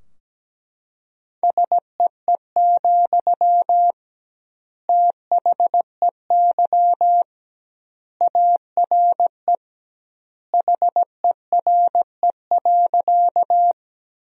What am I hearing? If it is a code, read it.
Morse code, heard as a keyed tone, about 17 wpm, 700 Hz. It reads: SEE, THEY ARE HERE.
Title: 2x Word-Spacing - Sentences from Top 200 Words - 17wpm